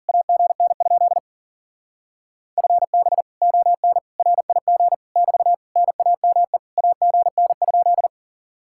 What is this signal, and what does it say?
Morse code, heard as a keyed tone, about 51 wpm, 700 Hz. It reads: AGN? FB ON RIG <BT> NAME AGN?